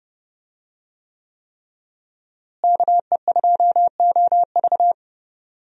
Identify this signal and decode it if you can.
Morse code, heard as a keyed tone, about 31 wpm, 700 Hz. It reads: KE2OV
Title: USA Call Signs - 5 Characters - Repeated 3x - 30wpm